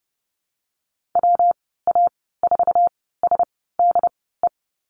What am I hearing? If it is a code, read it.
Morse code, heard as a keyed tone, about 30 wpm, spaced as for 17 wpm, 700 Hz. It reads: WA4SDE
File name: Lesson 19g - Letter w - Call Signs 30wpm